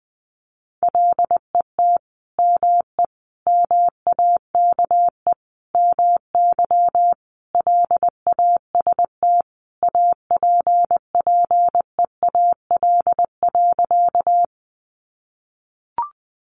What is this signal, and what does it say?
Morse code, heard as a keyed tone, about 19 wpm, 700 Hz. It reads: LET ME MAKE MY LAST APPEAL.
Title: Sentences from Top 1300 Words 20wpm